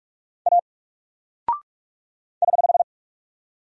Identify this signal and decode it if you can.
Morse code, heard as a keyed tone, about 44 wpm, 700 Hz. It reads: A <HH>
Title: Single Character (Letter:Number:Punctuation:Prosign) 45wpm